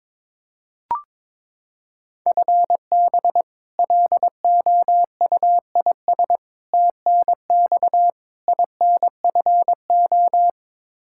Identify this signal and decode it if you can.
Morse code, heard as a keyed tone, about 22 wpm, 700 Hz. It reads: FB LOUIS TNX INFO
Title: K1USN SST QSO Practice 22wpm